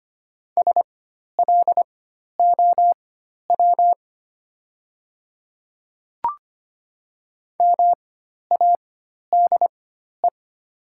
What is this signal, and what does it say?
Morse code, heard as a keyed tone, about 25 wpm, spaced as for 12 wpm, 700 Hz. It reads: SLOW MADE
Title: 4x Character Spacing - Top 500 Words - 25wpm